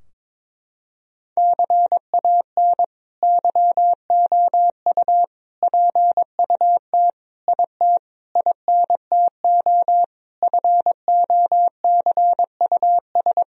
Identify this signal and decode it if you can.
Morse code, heard as a keyed tone, about 22 wpm, 700 Hz. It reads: CAN YOU PUT IT INTO FOCUS
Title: Sentences from Top 1000 Words - Review Part 4 - 22wpm